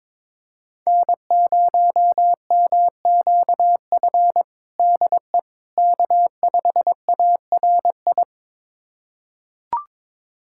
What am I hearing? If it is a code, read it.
Morse code, heard as a keyed tone, about 22 wpm, 700 Hz. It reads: N0MQF DE K5ARI
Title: USA Callsign DE Callsign 22wpm